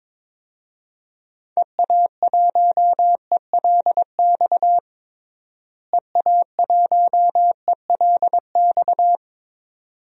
Morse code, heard as a keyed tone, about 22 wpm, 700 Hz. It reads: EA1ELX EA1ELX
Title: DXCC-100 Call Signs - 6 Characters - Repeated 2x - 22wpm